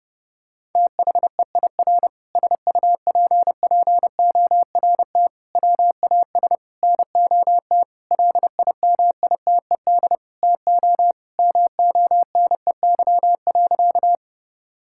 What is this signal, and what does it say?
Morse code, heard as a keyed tone, about 32 wpm, 700 Hz. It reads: THEIR SUPPORT WAS NOT LIMITED TO MONEY.
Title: Sentences from Top 1000 Words - Review Part 6 - 30wpm